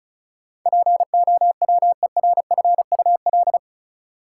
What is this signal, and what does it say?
Morse code, heard as a keyed tone, about 34 wpm, 700 Hz. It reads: POWERFUL
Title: Sentences from Top 1300 Words - New Word Drill Alt - 35wpm